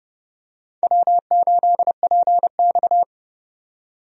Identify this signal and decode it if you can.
Morse code, heard as a keyed tone, about 30 wpm, 700 Hz. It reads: W8PX
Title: USA Call Signs - 4 Characters - 30wpm